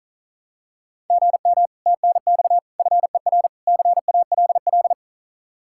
Morse code, heard as a keyed tone, about 39 wpm, 700 Hz. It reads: GM TNX FER CALL
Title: Speed-Racing Common QSO Phrases - 35wpm